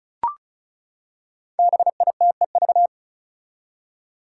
Morse code, heard as a keyed tone, about 36 wpm, 700 Hz. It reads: BITEV
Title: Five Challenging Letter-Number Combinations 35wpm